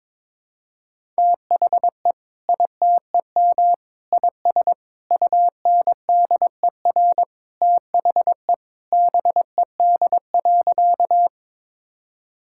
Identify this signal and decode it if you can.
Morse code, heard as a keyed tone, about 22 wpm, 700 Hz. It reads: THE ITEM IS UNDER THE BED.
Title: Sentences from Top 1000 Words - Review Part 6 - 22wpm